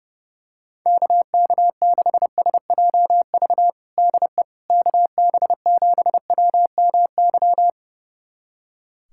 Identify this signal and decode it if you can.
Morse code, heard as a keyed tone, about 29 wpm, 700 Hz. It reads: KK6SJV DE KB7WMY
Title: USA Callsign DE Callsign - 6 Characters - 30wpm